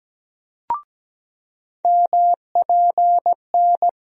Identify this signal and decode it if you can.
Morse code, heard as a keyed tone, about 17 wpm, 700 Hz. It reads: MPN
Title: USA Call Sign Postfixes - 3 Letters 17wpm